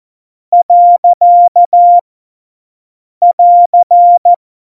Morse code, heard as a keyed tone, about 14 wpm, 700 Hz. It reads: . <AR>
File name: Morsekurs / Witze / Witz 18 August 2020, 50 F70